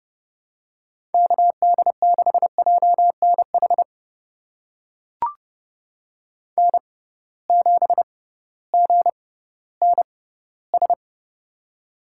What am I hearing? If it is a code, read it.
Morse code, heard as a keyed tone, about 30 wpm, spaced as for 16 wpm, 700 Hz. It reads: KD6JNH N7GNS
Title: USA Call Signs - 6x Character Spacing - 30wpm